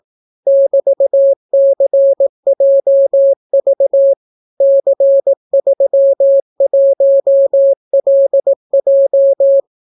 Morse code, heard as a keyed tone, about 18 wpm, 550 Hz. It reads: <BT>CJV C31LJ